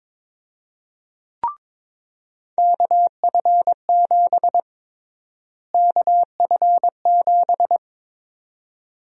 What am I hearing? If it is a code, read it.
Morse code, heard as a keyed tone, about 22 wpm, 700 Hz. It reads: KF7 KF7
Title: USA Call Sign Prefixes - Repeated 2x - 22wpm